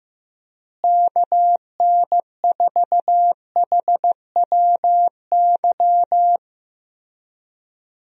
Morse code, heard as a keyed tone, about 15 wpm, 700 Hz. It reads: KN4HWY
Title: USA Call Signs - 6 Characters - Repeated 3x - 15wpm